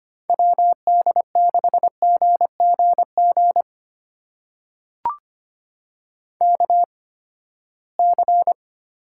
Morse code, heard as a keyed tone, about 25 wpm, spaced as for 17 wpm, 700 Hz. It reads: WD6GGG KC
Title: USA Call Signs - 6 Characters - 8x Character Spacing - 25wpm